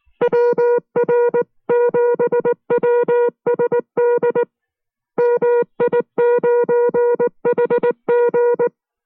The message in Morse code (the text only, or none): WR7WSD MI9HG